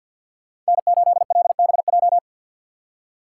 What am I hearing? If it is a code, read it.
Morse code, heard as a keyed tone, about 59 wpm, 700 Hz. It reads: N8LBJ
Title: Speed Racing - USA 1x3 Call Signs 50wpm